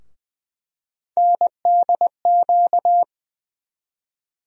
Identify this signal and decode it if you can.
Morse code, heard as a keyed tone, about 20 wpm, 700 Hz. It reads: NDQ
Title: USA Call Sign Postfixes - 3 Letters 20wpm